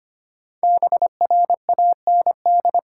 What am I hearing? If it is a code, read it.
Morse code, heard as a keyed tone, about 26 wpm, 700 Hz. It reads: BRAND
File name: 5 Letter Words - First 3 Letters Given - 25wpm